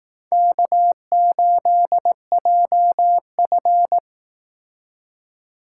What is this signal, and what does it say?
Morse code, heard as a keyed tone, about 18 wpm, 700 Hz. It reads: K8JF
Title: Speed Racing - USA 1x2 Call Signs 15wpm